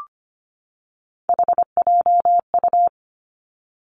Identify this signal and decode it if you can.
Morse code, heard as a keyed tone, about 25 wpm, 700 Hz. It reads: HJU
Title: Three Challenging Letter-Number Combinations 25wpm